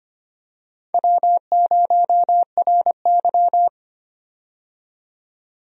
Morse code, heard as a keyed tone, about 25 wpm, 700 Hz. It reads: W0RY
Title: USA Call Signs - 4 Characters - 25wpm